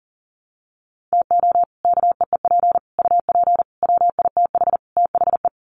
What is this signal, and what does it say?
Morse code, heard as a keyed tone, about 40 wpm, 700 Hz. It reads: TO KEEP UP WITH THE
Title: Sets of 5 Words Encore 40wpm